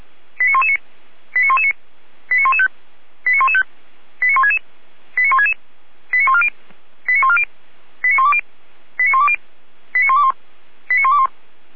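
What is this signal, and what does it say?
Five-tone selective call (ZVEI), heard as quick runs of five tones, tones of about 80 ms. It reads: 89119 89119 89115 89115 89166 89166 89140 89140 89120 89120 89121 89121
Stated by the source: ZVEI five-tone selective call received by SDR.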